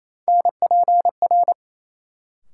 Morse code, heard as a keyed tone, about 28 wpm, 700 Hz. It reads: NPR
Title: Sets of 5 Words - New Word Drill - 28wpm